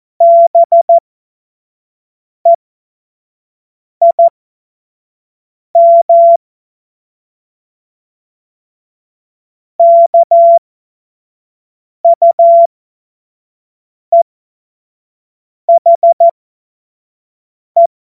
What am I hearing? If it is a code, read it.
Morse code, heard as a keyed tone, about 14 wpm, spaced as for 5 wpm, 680 Hz. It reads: BEIM KUEHE
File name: Morsekurs / Witze / Witz 07. September 2021, 25 F70